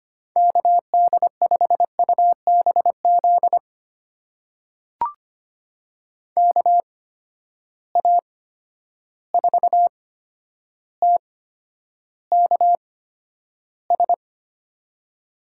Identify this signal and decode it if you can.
Morse code, heard as a keyed tone, about 25 wpm, spaced as for 11 wpm, 700 Hz. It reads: KD5UBZ KA4TKS
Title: USA Call Signs - 6 Characters - 8x Character Spacing - 25wpm